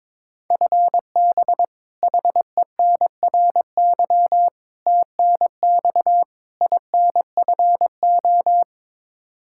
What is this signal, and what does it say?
Morse code, heard as a keyed tone, about 23 wpm, 700 Hz. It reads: FB HENRY TNX INFO
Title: Practice QSO - Exchange 3 - More Info - 22wpm